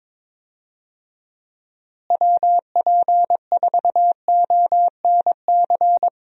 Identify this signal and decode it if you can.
Morse code, heard as a keyed tone, about 22 wpm, 700 Hz. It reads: WP4ONC
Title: USA Call Signs - 6 Characters - Repeated 3x - 22wpm